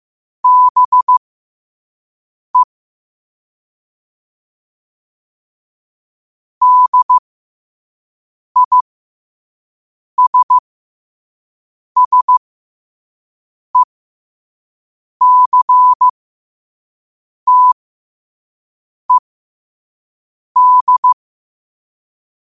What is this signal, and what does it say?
Morse code, heard as a keyed tone, about 15 wpm, spaced as for 5 wpm, 1000 Hz. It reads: BE DISSECTED